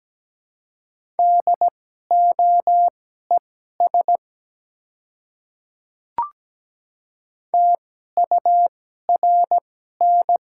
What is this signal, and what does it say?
Morse code, heard as a keyed tone, about 17 wpm, spaced as for 12 wpm, 700 Hz. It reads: DOES TURN
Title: 2x Character Spacing - Top 200 Words - 17wpm